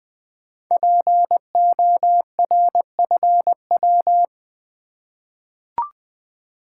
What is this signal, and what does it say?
Morse code, heard as a keyed tone, about 20 wpm, 700 Hz. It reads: PORFW
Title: Code Groups - 5 Letters - 20wpm